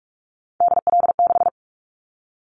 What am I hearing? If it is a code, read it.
Morse code, heard as a keyed tone, about 46 wpm, 700 Hz. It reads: DL6
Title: Callsign Prefixes for Top 50 DXCC - 45wpm